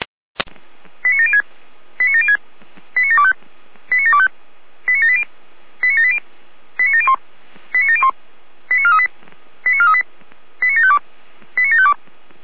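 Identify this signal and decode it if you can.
Five-tone selective call (ZVEI), heard as quick runs of five tones, tones of about 80 ms. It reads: 89706 89706 89725 89725 89780 89780 89701 89701 89438 89438 89752 89752
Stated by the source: Five-tone ZVEI selective call received by SDR.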